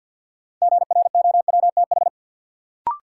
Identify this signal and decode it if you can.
Morse code, heard as a keyed tone, about 49 wpm, 700 Hz. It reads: GROWTH
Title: Sentences from Top 1000 Words - Review Part 5 - New Word Drill - 50wpm